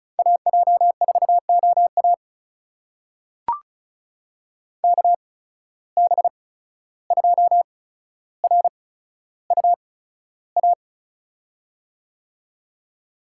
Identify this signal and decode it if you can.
Morse code, heard as a keyed tone, about 36 wpm, spaced as for 14 wpm, 700 Hz. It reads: AJ4OA KB2RUA
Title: USA Call Signs - 8x Character Spacing - 35wpm